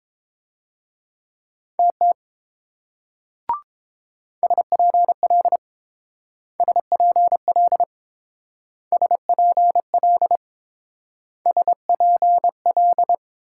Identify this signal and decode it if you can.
Morse code, heard as a keyed tone, about 28 wpm, 700 Hz. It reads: TT SPL SPL SPL SPL